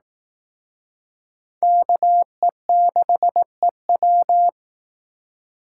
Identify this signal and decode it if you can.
Morse code, heard as a keyed tone, about 18 wpm, 700 Hz. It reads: KE6EW
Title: Speed Racing - USA 2x2 Call Signs 15wpm